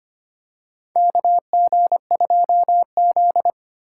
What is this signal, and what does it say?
Morse code, heard as a keyed tone, about 25 wpm, 690 Hz. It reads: KG2Z
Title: USA Call Signs - 4 Characters - 4x Character Spacing - 25wpm